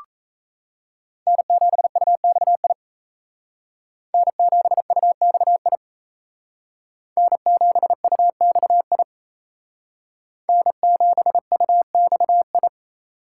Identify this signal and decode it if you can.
Morse code, heard as a keyed tone, about 35 wpm, 700 Hz. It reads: N7UXI N7UXI N7UXI N7UXI
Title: Speed Racing - USA Call Signs 28wpm